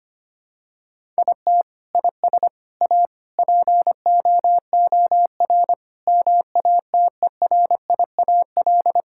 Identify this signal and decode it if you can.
Morse code, heard as a keyed tone, about 25 wpm, 700 Hz. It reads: IT IS A POOR MATERIAL
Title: Sentences from Top 1000 Words - Review Part 5 - 25wpm